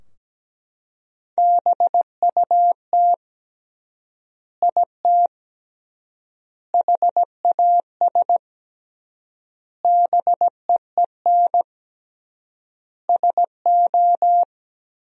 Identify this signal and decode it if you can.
Morse code, heard as a keyed tone, about 17 wpm, 700 Hz. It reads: BUT IT HAS BEEN SO
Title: 3x Word-Spacing - Sentences from Top 100 Words - 17wpm